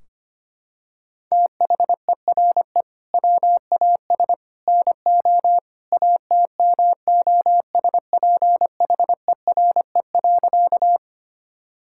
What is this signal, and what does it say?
Morse code, heard as a keyed tone, about 25 wpm, 700 Hz. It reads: THERE WAS NO ATMOSPHERE.
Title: Sentences from Top 1600 Words 25wpm